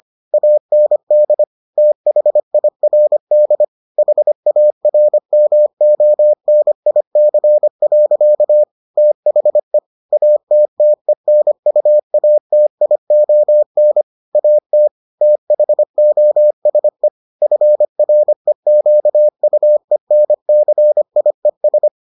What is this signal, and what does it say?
Morse code, heard as a keyed tone, about 25 wpm, 600 Hz. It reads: AND THIRD HARMONIC. THE ATTENUATION AT THOSE FREQUENCIES